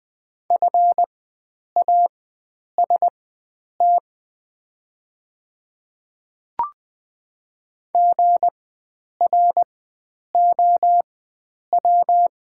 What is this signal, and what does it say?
Morse code, heard as a keyed tone, about 20 wpm, spaced as for 9 wpm, 700 Hz. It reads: FAST GROW
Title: 4x Character Spacing - Top 500 Words - 20wpm